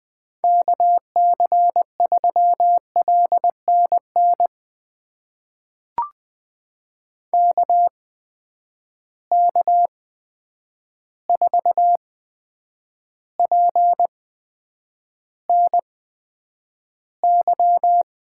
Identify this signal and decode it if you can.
Morse code, heard as a keyed tone, about 20 wpm, spaced as for 9 wpm, 700 Hz. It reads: KC3LNN KK4PNY
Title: USA Call Signs - 6 Characters - 8x Character Spacing - 20wpm